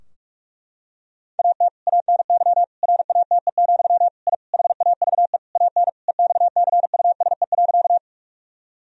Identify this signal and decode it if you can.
Morse code, heard as a keyed tone, about 46 wpm, 700 Hz. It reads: AT ANY RATE, I HAVE AN EXCUSE.
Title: Sentences from Top 1000 Words - Review Part 4 - 45wpm